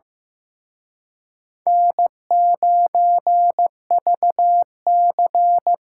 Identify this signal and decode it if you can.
Morse code, heard as a keyed tone, about 18 wpm, 700 Hz. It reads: N9VC
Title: Speed Racing - USA 1x2 Call Signs 15wpm